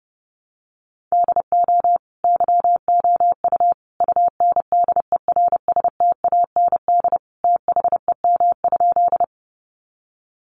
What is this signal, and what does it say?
Morse code, heard as a keyed tone, about 30 wpm, 700 Hz. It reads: DO YOU UNDERSTAND THEM?
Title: Sentences from Top 600 Words 30wpm